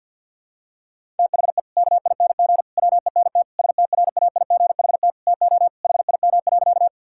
Morse code, heard as a keyed tone, about 50 wpm, 700 Hz. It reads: THE KING WENT STRAIGHT TO HIM.
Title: Sentences from Top 1000 Words - Review Part 6 - 50wpm